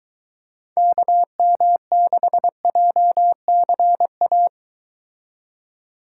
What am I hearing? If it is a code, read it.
Morse code, heard as a keyed tone, about 23 wpm, 700 Hz. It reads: KM6JCA